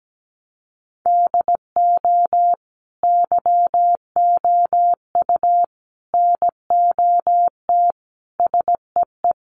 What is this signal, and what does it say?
Morse code, heard as a keyed tone, about 17 wpm, 700 Hz. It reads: DO YOU NOT SEE
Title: Sentences from Top 800 Words 17wpm